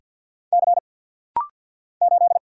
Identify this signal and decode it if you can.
Morse code, heard as a keyed tone, about 48 wpm, 690 Hz. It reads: C 8
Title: Single Letter-Number - Rapid-Fire 50wpm